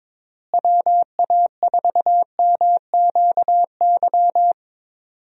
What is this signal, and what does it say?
Morse code, heard as a keyed tone, about 22 wpm, 700 Hz. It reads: WA4MQY
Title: USA Call Signs - 6 Characters - Repeated 3x - 22wpm